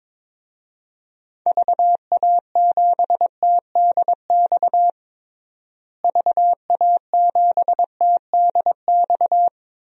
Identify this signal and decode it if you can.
Morse code, heard as a keyed tone, about 22 wpm, 700 Hz. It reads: VA7TDX VA7TDX
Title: DXCC-100 Call Signs - 6 Characters - Repeated 2x - 22wpm